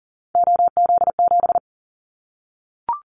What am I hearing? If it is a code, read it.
Morse code, heard as a keyed tone, about 40 wpm, 700 Hz. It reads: OZ7
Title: Callsign Prefixes for Top 100 DXCC - 40wpm